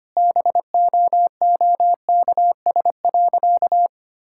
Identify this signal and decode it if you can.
Morse code, heard as a keyed tone, about 25 wpm, 700 Hz. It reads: BOOKS.
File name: practice / 25wpm / Difficult Sentences with Hints - 25wpm